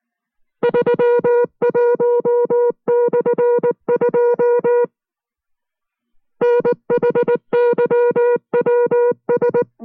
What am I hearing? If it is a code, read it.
Morse code, heard as a keyed tone, about 19 wpm, 480 Hz. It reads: 31/2 NHYWS